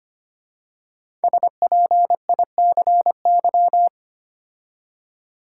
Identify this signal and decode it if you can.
Morse code, heard as a keyed tone, about 25 wpm, 700 Hz. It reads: SPICY
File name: 5 Letter Words - First 3 Letters Given - 25wpm